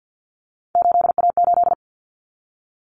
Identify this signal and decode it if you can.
Morse code, heard as a keyed tone, about 50 wpm, 700 Hz. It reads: 8A8
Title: Common Cut Numbers - 3-Digit 50wpm